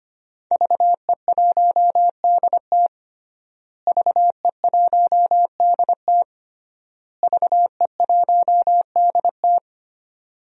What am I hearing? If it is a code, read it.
Morse code, heard as a keyed tone, about 25 wpm, 700 Hz. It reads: VE1DT VE1DT VE1DT
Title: DXCC-100 Call Signs - 5 Characters - Repeated 3x - 25wpm